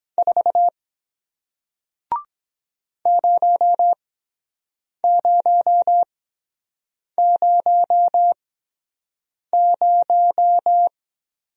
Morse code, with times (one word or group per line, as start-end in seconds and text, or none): 0.2-0.7 4
3.1-10.9 0000